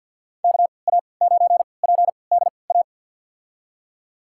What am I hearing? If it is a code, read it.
Morse code, heard as a keyed tone, about 53 wpm, spaced as for 29 wpm, 700 Hz. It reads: KA9PDA
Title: USA Call Signs - 6 Characters - 3x Character Spacing - 50wpm